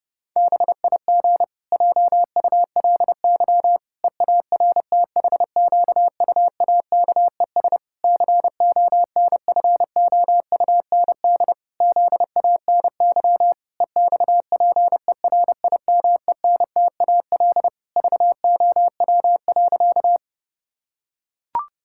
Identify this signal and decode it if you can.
Morse code, heard as a keyed tone, about 32 wpm, 700 Hz. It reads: BIG JULY EARTHQUAKES CONFOUND ZANY EXPERIMENTAL VOW.
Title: Pangram Challenge v2 - 30wpm